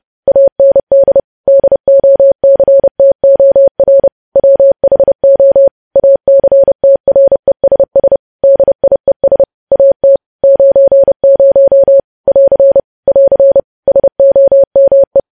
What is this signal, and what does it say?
Morse code, heard as a keyed tone, about 30 wpm, 550 Hz. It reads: AND DOCTOR WHO ACTRESS DIES AT 90 <AR> <AR> SOME
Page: news headlines in morse code at 30 wpm